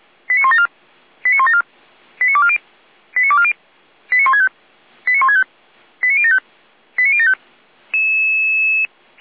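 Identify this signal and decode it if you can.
Five-tone selective call (ZVEI), heard as quick runs of five tones, tones of about 70 ms. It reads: 89174 89174 89240 89240 89165 89165 89075 89075
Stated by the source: ZVEI five-tone selective call received on SDR, server of feuerwehr-orlamuende (fire brigade Orlamünde).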